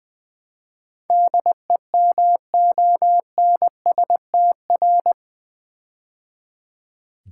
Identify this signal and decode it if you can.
Morse code, heard as a keyed tone, about 20 wpm, 700 Hz. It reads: DEMONSTR
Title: Build A Word - 4 Letters At A Time - 11-Letter Words - 20wpm